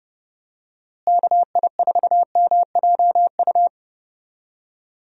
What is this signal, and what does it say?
Morse code, heard as a keyed tone, about 32 wpm, 700 Hz. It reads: KI4MJU
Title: USA Call Signs - 6 Characters - Repeated 3x - 30wpm